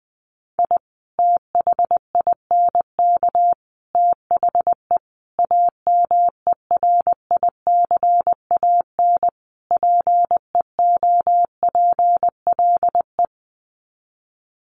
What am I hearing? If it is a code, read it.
Morse code, heard as a keyed tone, about 20 wpm, 700 Hz. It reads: I THINK THE AMERICAN PEOPLE
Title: Sets of 5 Words Encore 20wpm